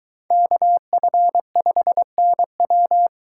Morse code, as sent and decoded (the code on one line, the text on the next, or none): -.- ..-. ..... -. .--
KF5NW